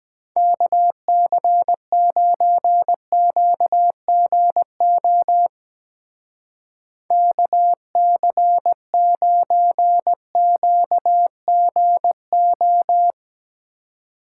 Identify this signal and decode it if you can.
Morse code, heard as a keyed tone, about 20 wpm, 700 Hz. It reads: KC9QGO KC9QGO
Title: Speed Racing - USA 2x3 Call Signs 17wpm